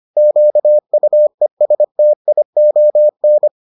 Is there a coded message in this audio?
Yes: Morse code from 0.2 to 3.5 s, heard as a keyed tone, about 25 wpm, 600 Hz. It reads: QUESTION